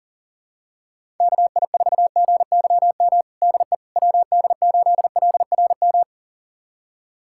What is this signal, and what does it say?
Morse code, heard as a keyed tone, about 40 wpm, 700 Hz. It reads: KI4GYM DE WD8LRM
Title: USA Callsign DE Callsign - 6 Characters - 40wpm